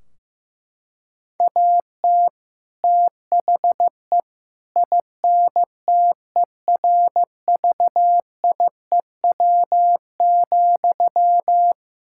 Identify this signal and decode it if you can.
Morse code, heard as a keyed tone, about 14 wpm, 700 Hz. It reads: AT THE INTERVIEW,
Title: Sentences from Top 1600 Words 15wpm